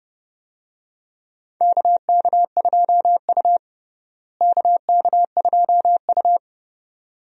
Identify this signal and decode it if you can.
Morse code, heard as a keyed tone, about 31 wpm, 700 Hz. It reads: KK2U KK2U
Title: USA Call Signs - 4 Characters - Repeated 2x - 30wpm